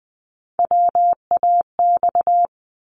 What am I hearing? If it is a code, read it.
Morse code, heard as a keyed tone, about 20 wpm, 700 Hz. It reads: WAX
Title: Multi-letter Phonemes 20wpm